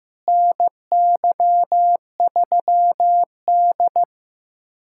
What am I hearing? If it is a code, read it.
Morse code, heard as a keyed tone, about 15 wpm, 700 Hz. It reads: NY3D
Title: USA Callsign DE Callsign - 4 Characters - 15wpm